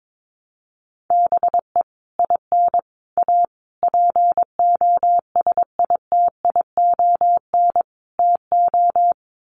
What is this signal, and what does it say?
Morse code, heard as a keyed tone, about 22 wpm, 700 Hz. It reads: BE IN A POSITION TO
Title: Sets of 5 Words Encore2 22wpm